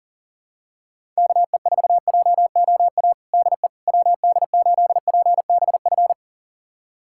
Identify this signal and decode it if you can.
Morse code, heard as a keyed tone, about 40 wpm, 700 Hz. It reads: KE4JOA DE WD8PBF
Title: USA Callsign DE Callsign - 6 Characters - 40wpm